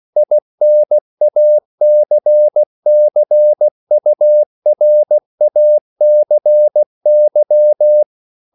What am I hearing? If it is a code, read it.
Morse code, heard as a keyed tone, about 16 wpm, 600 Hz. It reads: INACCURACY
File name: CW / Practice / EnglishQuotations0216